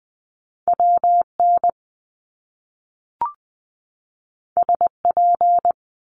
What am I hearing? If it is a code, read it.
Morse code, heard as a keyed tone, about 20 wpm, 700 Hz. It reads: W N S P
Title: Top Two Letter Combinations in Words 20wpm